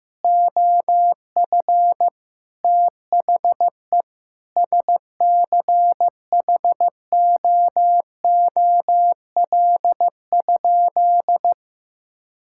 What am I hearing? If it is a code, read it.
Morse code, heard as a keyed tone, about 15 wpm, 700 Hz. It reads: OF THE SCHOOL?
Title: Sentences from Top 1000 Words - Review Part 5 - 15wpm